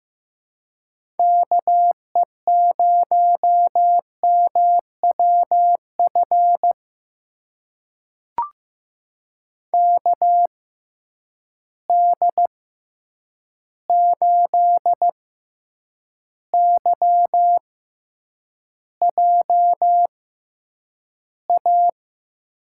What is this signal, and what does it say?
Morse code, heard as a keyed tone, about 15 wpm, spaced as for 8 wpm, 700 Hz. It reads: KE0MWF KD8YJA
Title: USA Call Signs - 6 Characters - 6x Character Spacing - 15wpm